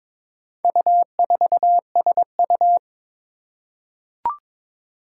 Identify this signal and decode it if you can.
Morse code, heard as a keyed tone, about 23 wpm, 700 Hz. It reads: U4SU
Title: Four Challenging Letter-Number Combinations 22wpm